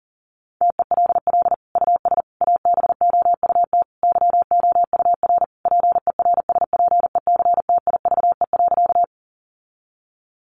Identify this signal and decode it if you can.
Morse code, heard as a keyed tone, about 40 wpm, 700 Hz. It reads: TELL US ABOUT YOUR PERSPECTIVE.
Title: Sentences from Top 1200 Words 40wpm